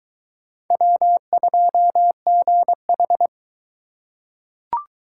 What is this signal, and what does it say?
Morse code, heard as a keyed tone, about 22 wpm, 700 Hz. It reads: W2GH